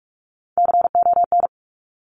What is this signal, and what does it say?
Morse code, heard as a keyed tone, about 46 wpm, 700 Hz. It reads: CON
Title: Top Three Letter Combinations in Words 45wpm